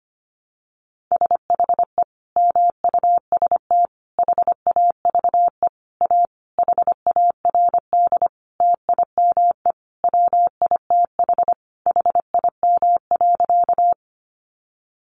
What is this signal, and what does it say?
Morse code, heard as a keyed tone, about 25 wpm, 700 Hz. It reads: SHE MUST HAVE A HARD TIME WITH HIM.
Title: Sentences from Top 300 Words 25wpm